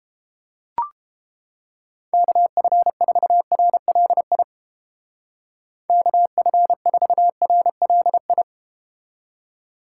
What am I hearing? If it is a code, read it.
Morse code, heard as a keyed tone, about 31 wpm, 700 Hz. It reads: KF4RLI KF4RLI